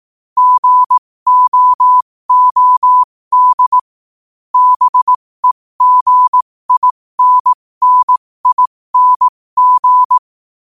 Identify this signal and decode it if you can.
Morse code, heard as a keyed tone, about 18 wpm, spaced as for 15 wpm, 1000 Hz. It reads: GOOD BEGINNING
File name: qotdfiles / 15wpm